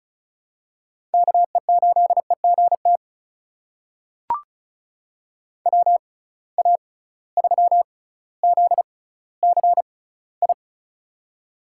Morse code, heard as a keyed tone, about 36 wpm, spaced as for 18 wpm, 700 Hz. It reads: KE8EGT WA3ZCI
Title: USA Call Signs - 6 Characters - 6x Character Spacing - 35wpm